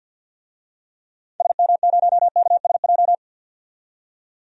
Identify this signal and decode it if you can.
Morse code, heard as a keyed tone, about 47 wpm, 700 Hz. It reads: SM0KSJ
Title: DXCC-100 Call Signs - 6 Characters - 50wpm